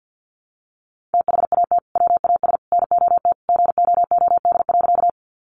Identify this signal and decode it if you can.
Morse code, heard as a keyed tone, about 50 wpm, 700 Hz. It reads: THAT WAS NOT GOOD.
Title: Sentences from Top 600 Words 50wpm